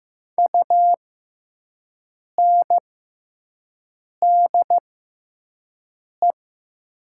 Morse code, heard as a keyed tone, about 15 wpm, spaced as for 5 wpm, 700 Hz. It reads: UNDE
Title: 6x Character Spacing - Top 200 Words - 15wpm